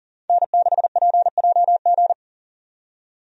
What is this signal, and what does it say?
Morse code, heard as a keyed tone, about 39 wpm, 700 Hz. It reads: N6PJG
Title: USA Call Signs - 5 Characters - 40wpm